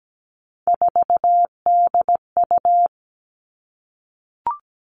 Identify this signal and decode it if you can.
Morse code, heard as a keyed tone, about 17 wpm, 700 Hz. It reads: 4DU
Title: Three Challenging Letter-Number Combinations 17wpm